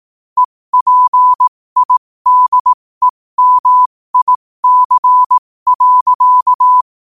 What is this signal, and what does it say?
Morse code, heard as a keyed tone, about 18 wpm, spaced as for 15 wpm, 1000 Hz. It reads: EPIDEMIC.
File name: qotdfiles / 15wpm